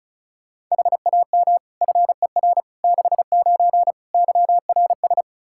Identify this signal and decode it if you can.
Morse code, heard as a keyed tone, about 34 wpm, 700 Hz. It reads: HAM FER 69 YRS
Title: K1USN SST QSO Practice 35wpm